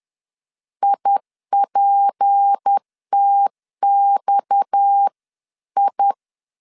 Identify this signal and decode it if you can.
Morse code, heard as a keyed tone, about 10 wpm, 800 Hz. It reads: IPTXI